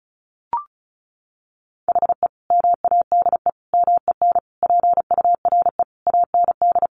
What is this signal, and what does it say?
Morse code, heard as a keyed tone, about 34 wpm, 700 Hz. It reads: HE MADE MEN PURE AND
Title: Sentences from Top 1500 Words 35wpm